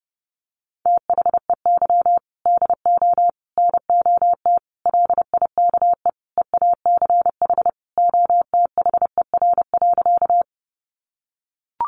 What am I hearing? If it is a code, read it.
Morse code, heard as a keyed tone, about 30 wpm, 700 Hz. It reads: THEY DO NOT LIKE EACH OTHER.
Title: Easy Phrases 30wpm